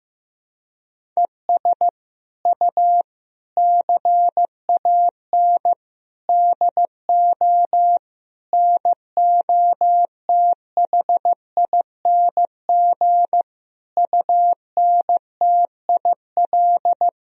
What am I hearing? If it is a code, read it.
Morse code, heard as a keyed tone, about 15 wpm, 700 Hz. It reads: ES U CAN DO NOTHING UNTIL
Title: Sentences from Top 500 Words and Abbreviations 15wpm